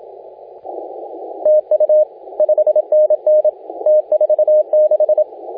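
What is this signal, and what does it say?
Morse code, heard as a keyed tone, about 27 wpm, 610 Hz, with radio noise. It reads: TU 5NN T46
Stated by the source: Morse contest audio recorded by LA4C during the SAC CW 2018 contest.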